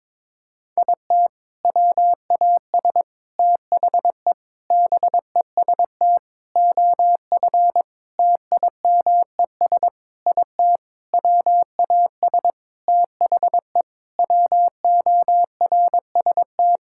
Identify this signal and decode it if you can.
Morse code, heard as a keyed tone, about 22 wpm, 700 Hz. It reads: IT WAS THE BEST OF TIMES IT WAS THE WORST
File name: practice / 22wpm / Sentences with 8-Word Phrases - 22wpm